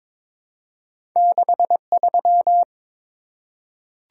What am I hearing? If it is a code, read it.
Morse code, heard as a keyed tone, about 22 wpm, 700 Hz. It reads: 63
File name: practice / 22wpm / Two Letter-Number Combinations - 22wpm